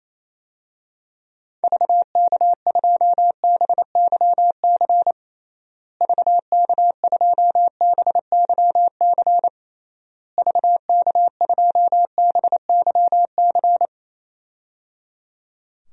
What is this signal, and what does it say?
Morse code, heard as a keyed tone, about 27 wpm, 700 Hz. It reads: VK2BYC VK2BYC VK2BYC
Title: DXCC-100 Call Signs - 6 Characters - Repeated 3x - 28wpm